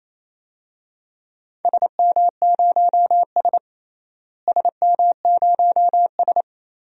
Morse code, heard as a keyed tone, about 28 wpm, 700 Hz. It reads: SM0S SM0S
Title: DXCC-100 Call Signs - 4 Characters - Repeated 2x - 28wpm